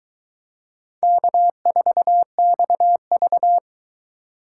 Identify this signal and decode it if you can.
Morse code, heard as a keyed tone, about 22 wpm, 700 Hz. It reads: K4XV